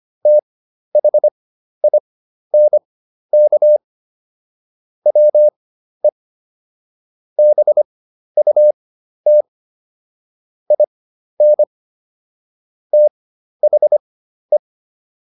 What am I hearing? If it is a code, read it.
Morse code, heard as a keyed tone, about 25 wpm, spaced as for 12 wpm, 600 Hz. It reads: THINK WE BUT IN THE